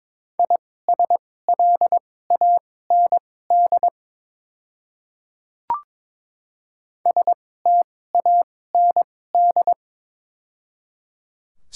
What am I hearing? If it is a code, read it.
Morse code, heard as a keyed tone, about 22 wpm, spaced as for 16 wpm, 700 Hz. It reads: ISLAND STAND
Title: 2x Character Spacing - Top 500 Words - 22wpm